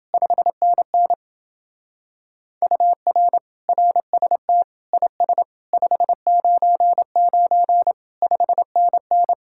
Morse code, heard as a keyed tone, about 28 wpm, 700 Hz. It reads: 5NN UR RST IS 599 5NN